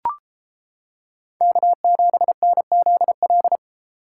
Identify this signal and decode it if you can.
Morse code, heard as a keyed tone, about 31 wpm, 700 Hz. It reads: K7NZL